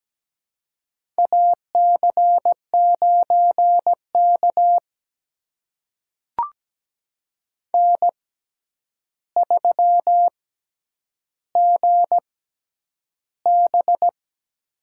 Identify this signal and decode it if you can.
Morse code, heard as a keyed tone, about 17 wpm, spaced as for 9 wpm, 700 Hz. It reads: AC9K N3GB
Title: USA Call Signs - 4 Characters - 6x Character Spacing - 17wpm